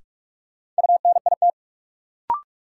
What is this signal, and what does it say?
Morse code, heard as a keyed tone, about 46 wpm, 700 Hz. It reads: UNIT
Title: Sentences from Top 1000 Words - Review Part 2 - New Word Drill - 45wpm